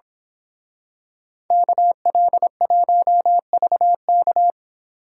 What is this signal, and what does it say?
Morse code, heard as a keyed tone, about 26 wpm, 700 Hz. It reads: KL1VK
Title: Speed Racing - USA 2x2 Call Signs 22wpm